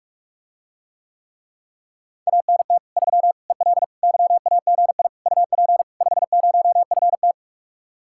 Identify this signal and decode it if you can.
Morse code, heard as a keyed tone, about 44 wpm, 700 Hz. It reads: ANT 3 EL YAGI UP 50FT